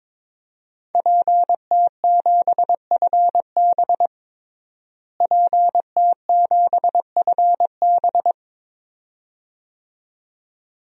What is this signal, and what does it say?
Morse code, heard as a keyed tone, about 22 wpm, 700 Hz. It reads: PT7FB PT7FB
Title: DXCC-100 Call Signs - 5 Characters - Repeated 2x - 22wpm